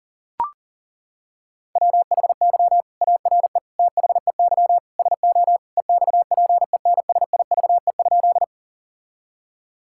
Morse code, heard as a keyed tone, about 40 wpm, 700 Hz. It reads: WHY ARE THEY SO EXPENSIVE?
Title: Sentences from Top 1000 Words - Review Part 6 - 40wpm